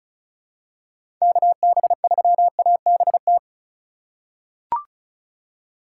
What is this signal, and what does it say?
Morse code, heard as a keyed tone, about 36 wpm, 700 Hz. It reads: KB3ABT
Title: USA Call Signs - 6 Characters - 3x Character Spacing - 35wpm